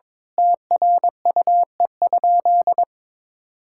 Morse code, heard as a keyed tone, about 22 wpm, 700 Hz. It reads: TRUE?
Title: Sentences from Top 1000 Words - Review Part 6 - 22wpm